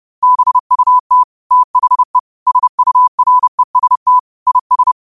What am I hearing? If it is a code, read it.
Morse code, heard as a keyed tone, about 30 wpm, 1000 Hz. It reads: BUT THE SUREST IS
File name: qotdfiles / 30wpm